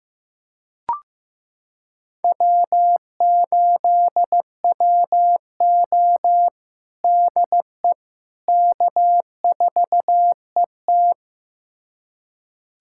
Morse code, heard as a keyed tone, about 15 wpm, 700 Hz. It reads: W8WO DE K4ET
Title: USA Callsign DE Callsign - 4 Characters - 15wpm